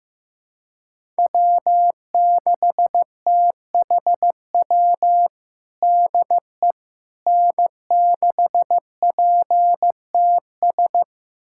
Morse code, heard as a keyed tone, about 15 wpm, 700 Hz. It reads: W6THW DE N6PTS
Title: USA Callsign DE Callsign - 5 Characters - 15wpm